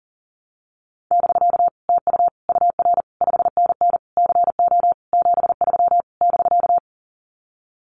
Morse code, heard as a keyed tone, about 40 wpm, 700 Hz. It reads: <BK> TU UR 5NN CO 73 <BK>
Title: Common QSO Phrases Encore 40wpm